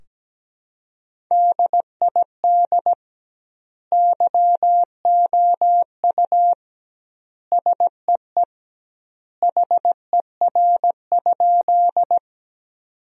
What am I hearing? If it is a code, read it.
Morse code, heard as a keyed tone, about 17 wpm, 700 Hz. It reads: DID YOU SEE HER?
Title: 2x Word-Spacing - Sentences from Top 100 Words - 17wpm